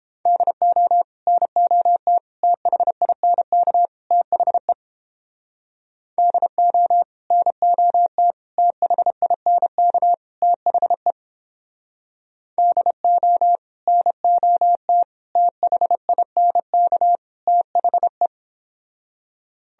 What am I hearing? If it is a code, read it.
Morse code, heard as a keyed tone, about 30 wpm, 700 Hz. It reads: DO NOT THINK THE DO NOT THINK THE DO NOT THINK THE